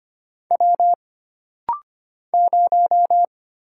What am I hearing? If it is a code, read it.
Morse code, heard as a keyed tone, about 25 wpm, 700 Hz. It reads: W 0
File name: Single Character (Letter:Number:Punctuation:Prosign) - Rapid-Fire - 25wpm